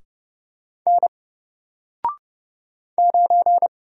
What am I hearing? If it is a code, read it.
Morse code, heard as a keyed tone, about 30 wpm, 700 Hz. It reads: N 9
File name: Single-Letter-Number-30wpm